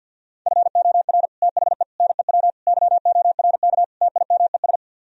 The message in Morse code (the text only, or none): FOR THE NEW YORK TIMES